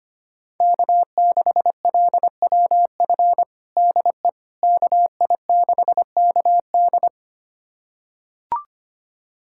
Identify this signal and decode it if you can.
Morse code, heard as a keyed tone, about 25 wpm, 700 Hz. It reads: K6LWF DE KI6KD
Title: USA Callsign DE Callsign - 5 Characters - 25wpm